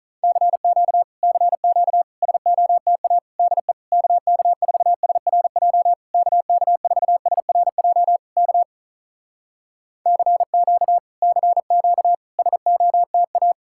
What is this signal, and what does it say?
Morse code, heard as a keyed tone, about 45 wpm, 700 Hz. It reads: CQ CQ SOTA DE KK4SRJ KK4SRJ K CQ CQ SOTA